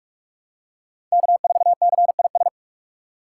Morse code, heard as a keyed tone, about 46 wpm, 700 Hz. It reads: K4CIS
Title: USA Call Signs - 5 Characters - 6x Character Spacing - 45wpm